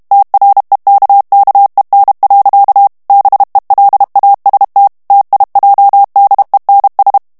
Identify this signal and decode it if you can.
Morse code, heard as a keyed tone, about 32 wpm, 800 Hz. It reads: TREKKEN. BELAST TIJDENS
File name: HOLLANDAIS texte clair 2 vitesse 30 mots minute